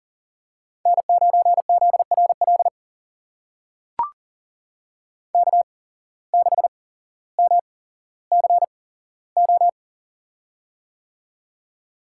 Morse code, heard as a keyed tone, about 40 wpm, spaced as for 17 wpm, 700 Hz. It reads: N9ZRL K6MCO
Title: USA Call Signs - 5 Characters - 8x Character Spacing - 40wpm